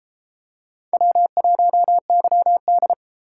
Morse code, heard as a keyed tone, about 32 wpm, 700 Hz. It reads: W1YD